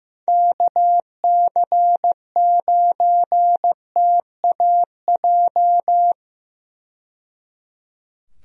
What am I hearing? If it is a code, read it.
Morse code, heard as a keyed tone, about 15 wpm, 700 Hz. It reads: KC9TAJ
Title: USA Call Signs - 6 Characters - 15wpm